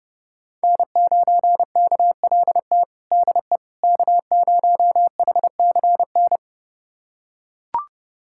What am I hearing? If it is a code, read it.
Morse code, heard as a keyed tone, about 30 wpm, 700 Hz. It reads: N9KLT DE K0HCN
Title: USA Callsign DE Callsign - 5 Characters - 30wpm